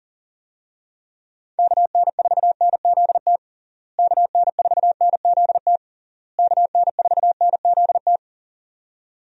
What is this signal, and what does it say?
Morse code, heard as a keyed tone, about 37 wpm, 700 Hz. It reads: KN4NZT KN4NZT KN4NZT
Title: USA Call Signs - 6 Characters - Repeated 3x - 40wpm